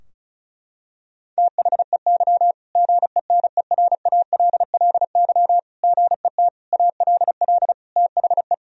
Morse code, heard as a keyed tone, about 36 wpm, 700 Hz. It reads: THEY GENERALLY GET ALL THE
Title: Sentences from Top 1000 Words - Review Part 4 - 35wpm